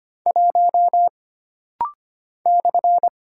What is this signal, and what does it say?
Morse code, heard as a keyed tone, about 25 wpm, 700 Hz. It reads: 1 /
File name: Single Character (Letter:Number:Punctuation:Prosign) - Rapid-Fire - 25wpm